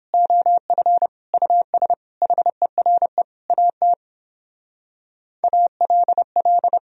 Morse code, heard as a keyed tone, about 29 wpm, 700 Hz. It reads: OF US HERE AT ALL